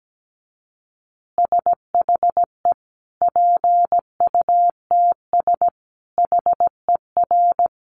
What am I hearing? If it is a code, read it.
Morse code, heard as a keyed tone, about 17 wpm, 700 Hz. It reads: SHE PUTS HER
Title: Easy Phrases 17wpm